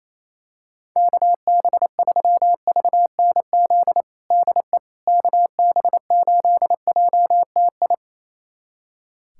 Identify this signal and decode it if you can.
Morse code, heard as a keyed tone, about 28 wpm, 700 Hz. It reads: KB3VNZ DE KB8JTI
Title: USA Callsign DE Callsign 28wpm